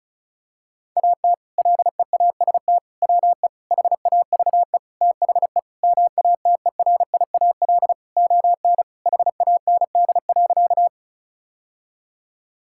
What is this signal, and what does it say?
Morse code, heard as a keyed tone, about 34 wpm, 700 Hz. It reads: AT LEAST WE HAVE THE MATERIAL ON HAND.
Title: Sentences from Top 1000 Words - Review Part 5 - 35wpm